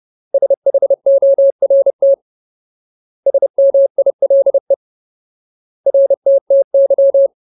Morse code, heard as a keyed tone, about 30 wpm, 550 Hz. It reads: SHORT SMILE RTTY
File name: Ultimate Word List / 30WPM Ultimate